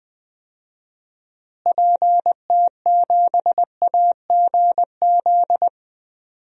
Morse code, heard as a keyed tone, about 20 wpm, 700 Hz. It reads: PT7AGZ
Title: DXCC-100 Call Signs - 6 Characters - Repeated 3x - 20wpm